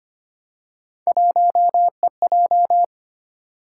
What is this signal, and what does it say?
Morse code, heard as a keyed tone, about 25 wpm, 700 Hz. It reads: 1EJ
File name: Three Letter-Number Combinations - 25wpm